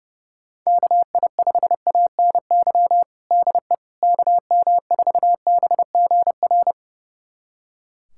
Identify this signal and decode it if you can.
Morse code, heard as a keyed tone, about 30 wpm, 700 Hz. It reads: KI5ANY DE KM4BGR
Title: USA Callsign DE Callsign 30wpm